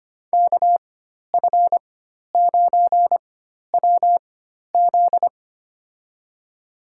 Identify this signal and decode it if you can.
Morse code, heard as a keyed tone, about 25 wpm, spaced as for 12 wpm, 700 Hz. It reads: KF9WZ
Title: USA Call Signs - 5 Characters - 4x Character Spacing - 25wpm